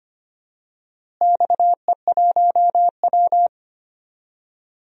Morse code, heard as a keyed tone, about 25 wpm, 700 Hz. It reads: XE1W
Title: DXCC-100 Call Signs - 4 Characters - Repeated 2x - 25wpm